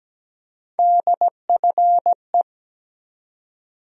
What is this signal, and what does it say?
Morse code, heard as a keyed tone, about 17 wpm, 700 Hz. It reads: DFE
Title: USA Call Sign Postfixes - 3 Letters 17wpm